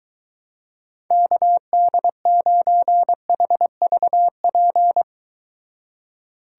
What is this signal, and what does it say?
Morse code, heard as a keyed tone, about 23 wpm, 700 Hz. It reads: KD9HVP